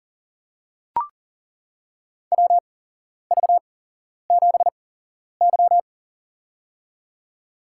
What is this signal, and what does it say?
Morse code, heard as a keyed tone, about 39 wpm, spaced as for 11 wpm, 700 Hz. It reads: WV7Y
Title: USA Call Signs - 4 Characters - 8x Character Spacing - 40wpm